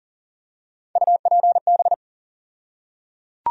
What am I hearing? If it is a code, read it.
Morse code, heard as a keyed tone, about 37 wpm, 700 Hz. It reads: UPB
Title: USA Call Sign Postfixes - 3 Letters 40wpm